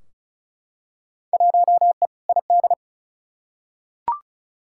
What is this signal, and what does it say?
Morse code, heard as a keyed tone, about 34 wpm, 700 Hz. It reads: 1E ID
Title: Field Day Exchanges - 35wpm